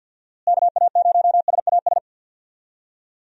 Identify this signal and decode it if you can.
Morse code, heard as a keyed tone, about 51 wpm, 700 Hz. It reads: KA0SAS
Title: USA Call Signs - 6 Characters - Repeated 3x - 50wpm